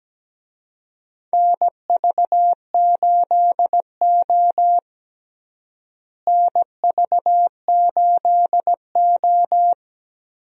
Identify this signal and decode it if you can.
Morse code, heard as a keyed tone, about 17 wpm, 700 Hz. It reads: NV8O NV8O
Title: USA Call Signs - 4 Characters - Repeated 3x - 17wpm